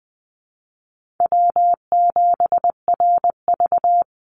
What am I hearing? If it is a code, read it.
Morse code, heard as a keyed tone, about 20 wpm, 700 Hz. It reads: W7RV
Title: NA SOTA Chaser Callsigns 20wpm